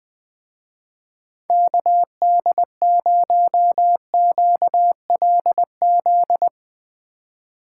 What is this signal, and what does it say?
Morse code, heard as a keyed tone, about 20 wpm, 700 Hz. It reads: KD0QLZ
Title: Speed Racing - USA 2x3 Call Signs 17wpm